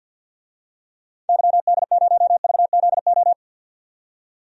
Morse code, heard as a keyed tone, about 52 wpm, 700 Hz. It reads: YD0VZO
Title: DXCC-100 Call Signs - 6 Characters - 50wpm